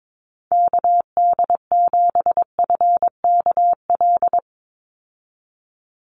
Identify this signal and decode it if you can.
Morse code, heard as a keyed tone, about 22 wpm, 700 Hz. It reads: KD7FKL
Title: USA Call Signs 22wpm